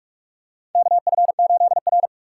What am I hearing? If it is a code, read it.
Morse code, heard as a keyed tone, about 46 wpm, 700 Hz. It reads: KF8R
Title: USA Call Signs - 4x Character Spacing - 45wpm